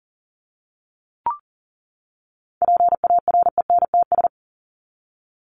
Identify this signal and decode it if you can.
Morse code, heard as a keyed tone, about 40 wpm, 700 Hz. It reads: PARENTS
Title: ES:S Suffix - Top 100 Words with Base Words 40wpm